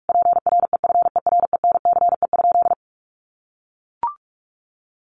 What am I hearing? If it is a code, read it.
Morse code, heard as a keyed tone, about 44 wpm, 700 Hz. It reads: PREFERENCE?
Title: Sentences from Top 1500 Words 45wpm